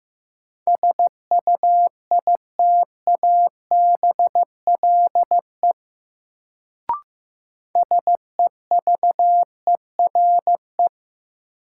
Morse code, heard as a keyed tone, about 15 wpm, 700 Hz. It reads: SUITABLE SEVERE
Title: Sentences from Top 1400 Words - New Word Drill - 15wpm